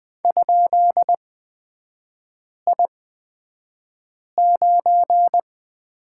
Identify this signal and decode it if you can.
Morse code, heard as a keyed tone, about 20 wpm, spaced as for 5 wpm, 700 Hz. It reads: ?I9
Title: 224 Lesson 36e - BT - Three Characters 20wpm